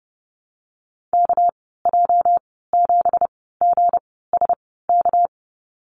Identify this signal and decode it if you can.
Morse code, heard as a keyed tone, about 30 wpm, spaced as for 17 wpm, 700 Hz. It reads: KJ7GSK
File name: Lesson 37g - Letter x - Call Signs 30wpm